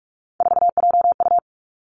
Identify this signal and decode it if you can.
Morse code, heard as a keyed tone, about 46 wpm, 700 Hz. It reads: 4JU
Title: Three Challenging Letter-Number Combinations 45wpm